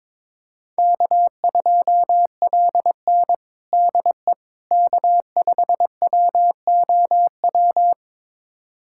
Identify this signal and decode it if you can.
Morse code, heard as a keyed tone, about 22 wpm, 700 Hz. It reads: K2LN DE K5WOW
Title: USA Callsign DE Callsign 22wpm